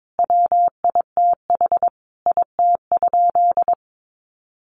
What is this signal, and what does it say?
Morse code, heard as a keyed tone, about 22 wpm, 700 Hz. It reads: WITH IT?
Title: Sentences from Top 600 Words 22wpm